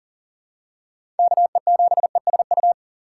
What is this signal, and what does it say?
Morse code, heard as a keyed tone, about 40 wpm, 700 Hz. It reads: KE7ESU
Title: USA Call Signs - 6 Characters - 3x Character Spacing - 40wpm